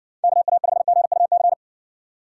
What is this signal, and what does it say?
Morse code, heard as a keyed tone, about 53 wpm, 700 Hz. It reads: KA5GUC